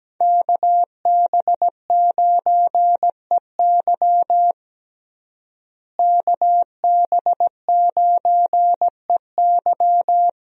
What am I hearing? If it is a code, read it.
Morse code, heard as a keyed tone, about 17 wpm, 700 Hz. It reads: KB9EY KB9EY
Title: USA Call Signs - 5 Characters - Repeated 2x - 17wpm